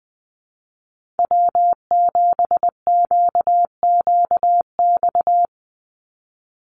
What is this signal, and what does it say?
Morse code, heard as a keyed tone, about 20 wpm, 700 Hz. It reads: W7QQX
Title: USA Call Signs 20wpm